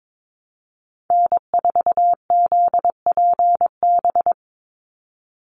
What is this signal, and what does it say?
Morse code, heard as a keyed tone, about 22 wpm, 700 Hz. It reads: N4ZPB
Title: USA Call Signs 22wpm